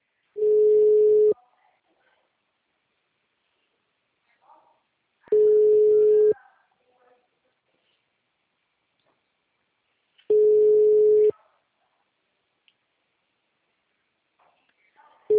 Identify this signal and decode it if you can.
Ringback tone (Europe, 425 Hz), with speech.